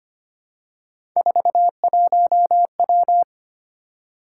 Morse code, heard as a keyed tone, about 25 wpm, 700 Hz. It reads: 41W
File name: Three Letter-Number Combinations - 25wpm